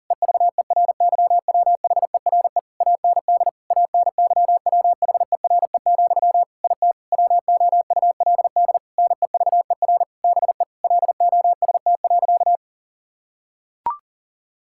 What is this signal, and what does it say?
Morse code, heard as a keyed tone, about 40 wpm, 700 Hz. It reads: EVERYWHERE AND ANYWHERE, IT WOULD NEVER BE LOST.
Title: Sentences from Top 1600 Words 40wpm